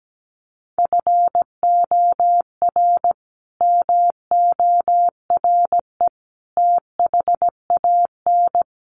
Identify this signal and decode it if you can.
Morse code, heard as a keyed tone, about 17 wpm, 700 Hz. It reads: FOR MORE THAN
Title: Sets of 3 Words 17wpm